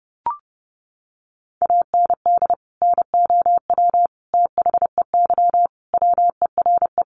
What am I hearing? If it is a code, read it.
Morse code, heard as a keyed tone, about 30 wpm, 700 Hz. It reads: AND NOW THEY WERE
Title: Sentences from Top 1100 Words 30wpm